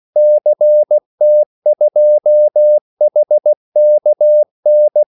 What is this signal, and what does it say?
Morse code, heard as a keyed tone, about 16 wpm, 600 Hz. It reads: CT2HKN
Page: Kaldesignaler 16 wpm